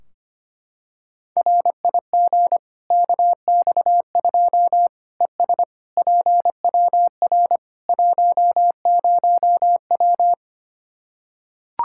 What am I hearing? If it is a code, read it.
Morse code, heard as a keyed tone, about 25 wpm, 700 Hz. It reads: RIG KX2 ES PWR 10W
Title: K1USN SST QSO Practice 25wpm